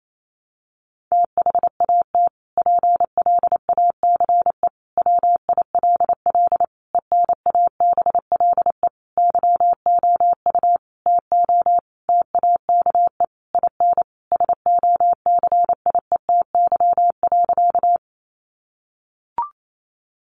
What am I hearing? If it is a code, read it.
Morse code, heard as a keyed tone, about 28 wpm, 700 Hz. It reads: THAT PLACE WILL ENABLE YOU TO TAKE IN SOCIETY.
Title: Sentences from Top 1500 Words 28wpm